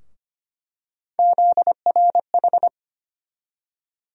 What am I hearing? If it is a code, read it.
Morse code, heard as a keyed tone, about 25 wpm, 700 Hz. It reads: ZRH
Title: USA Call Sign Postfixes - 3 Letters 25wpm